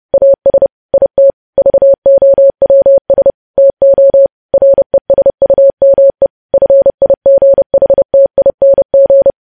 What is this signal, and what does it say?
Morse code, heard as a keyed tone, about 30 wpm, 550 Hz. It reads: AS IT VOWS TO RESUME FIGHTING
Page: news headlines in morse code at 30 wpm